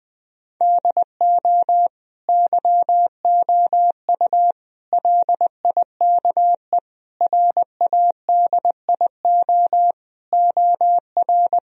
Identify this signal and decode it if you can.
Morse code, heard as a keyed tone, about 20 wpm, 700 Hz. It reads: DO YOU LIKE RADIO OR
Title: Sentences from Top 1000 Words - Review Part 4 - 20wpm